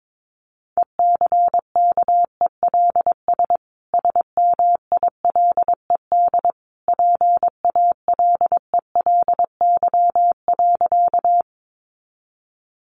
Morse code, heard as a keyed tone, about 22 wpm, 700 Hz. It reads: ECKELS SMILED PALELY.
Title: A Sound Of Thunder 22wpm